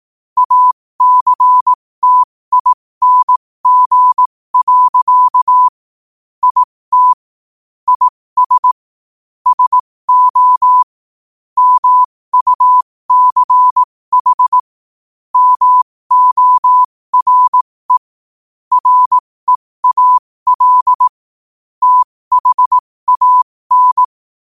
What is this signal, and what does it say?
Morse code, heard as a keyed tone, about 18 wpm, spaced as for 15 wpm, 1000 Hz. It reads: ACTING. IT IS SO MUCH MORE REAL THAN
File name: qotdfiles / 15wpm